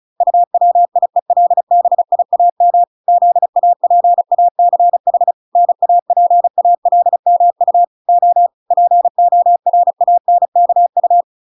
Morse code, heard as a keyed tone, about 34 wpm, 700 Hz. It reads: UWIELBIAM ZAPACH NAPALMU O PORANKU